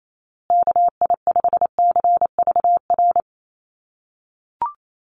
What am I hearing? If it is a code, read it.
Morse code, heard as a keyed tone, about 28 wpm, 700 Hz. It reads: KI5CVR
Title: USA Call Signs Repeated 2x 28wpm